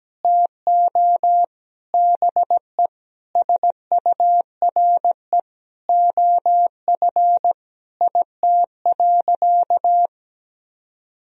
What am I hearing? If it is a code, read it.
Morse code, heard as a keyed tone, about 17 wpm, 700 Hz. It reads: TO BE SURE OF IT.
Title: Sentences from Top 1000 Words - Review Part 4 - 17wpm